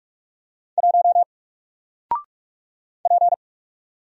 Morse code, heard as a keyed tone, about 45 wpm, 700 Hz. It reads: 1 P
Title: Single Letter-Number Alt 45wpm